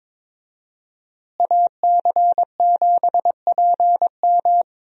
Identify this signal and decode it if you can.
Morse code, heard as a keyed tone, about 22 wpm, 700 Hz. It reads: AC7PM
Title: USA Call Signs - 5 Characters - Repeated 2x - 22wpm